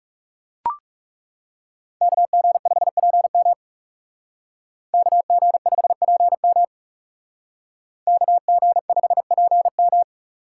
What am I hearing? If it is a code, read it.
Morse code, heard as a keyed tone, about 38 wpm, 700 Hz. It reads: KG5PM KG5PM KG5PM